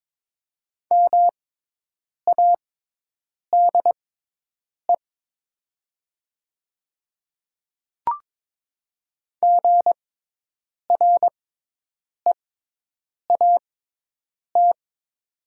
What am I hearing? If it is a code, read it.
Morse code, heard as a keyed tone, about 22 wpm, spaced as for 8 wpm, 700 Hz. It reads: MADE GREAT
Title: 6x Character Spacing - Top 200 Words - 22wpm